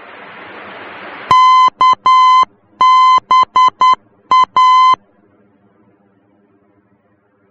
Morse code, heard as a keyed tone, about 10 wpm, 1020 Hz. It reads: KBA
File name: germany / mx / KBA-D